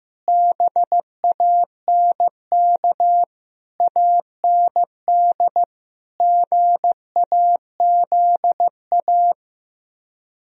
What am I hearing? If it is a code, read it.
Morse code, heard as a keyed tone, about 15 wpm, 700 Hz. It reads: BANK AND GAZA
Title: Speed-Racing - Sets of 5 Words 15wpm